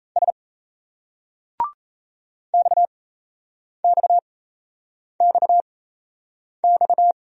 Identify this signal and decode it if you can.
Morse code, heard as a keyed tone, about 36 wpm, spaced as for 8 wpm, 700 Hz. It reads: S XXXX